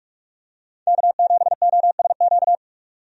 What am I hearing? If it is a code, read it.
Morse code, heard as a keyed tone, about 47 wpm, 700 Hz. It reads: K7OSQ
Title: USA Call Signs - 5 Characters - 45wpm